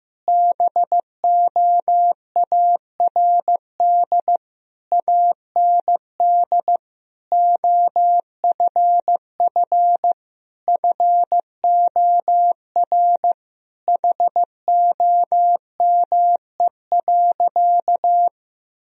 Morse code, heard as a keyed tone, about 15 wpm, 700 Hz. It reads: BOARD AND OFF FOR HOME.
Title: Sentences from Top 1000 Words - Review Part 4 - 15wpm